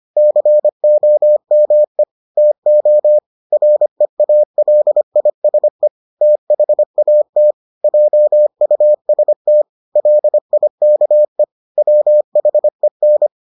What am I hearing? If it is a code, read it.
Morse code, heard as a keyed tone, about 25 wpm, 600 Hz. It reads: COME TO REALISE THAT JUST LIKE WHEN